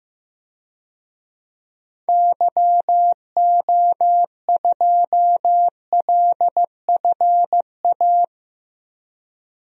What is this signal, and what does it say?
Morse code, heard as a keyed tone, about 15 wpm, 700 Hz. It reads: YO2LFA
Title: DXCC-100 Call Signs - 6 Characters - Repeated 2x - 15wpm